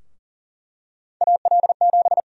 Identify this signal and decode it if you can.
Morse code, heard as a keyed tone, about 40 wpm, 700 Hz. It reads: AL7
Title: USA Call Sign Prefixes - 40wpm